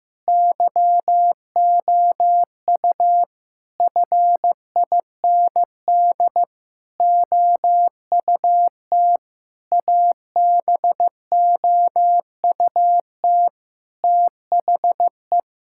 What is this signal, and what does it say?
Morse code, heard as a keyed tone, about 15 wpm, 700 Hz. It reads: YOU FIND OUT ABOUT THE
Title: Sentences from Top 1000 Words - Review Part 5 - 15wpm